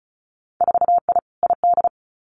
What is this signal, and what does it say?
Morse code, heard as a keyed tone, about 36 wpm, 700 Hz. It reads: 4I ID
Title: Winter Field Day Exchanges 35wpm